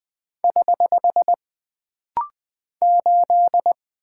Morse code, heard as a keyed tone, about 20 wpm, 700 Hz. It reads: <HH> 8
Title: Single Character (Letter:Number:Punctuation:Prosign) - Rapid-Fire - 20wpm